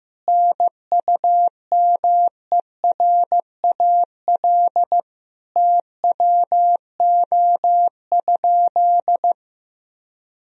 Morse code, heard as a keyed tone, about 15 wpm, 700 Hz. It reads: NUMERAL TWO?
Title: Sentences from Top 1000 Words - Review Part 2 - 15wpm